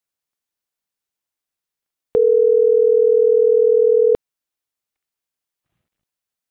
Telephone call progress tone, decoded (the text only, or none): ringback tone